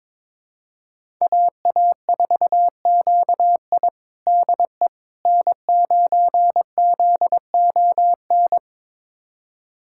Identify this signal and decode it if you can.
Morse code, heard as a keyed tone, about 22 wpm, 700 Hz. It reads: AA4QI DE N9ZON
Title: USA Callsign DE Callsign - 5 Characters - 22wpm